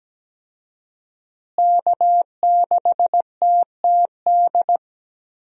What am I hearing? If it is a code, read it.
Morse code, heard as a keyed tone, about 17 wpm, 700 Hz. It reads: K6TTD
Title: USA Call Signs - 5 Characters - Repeated 2x - 17wpm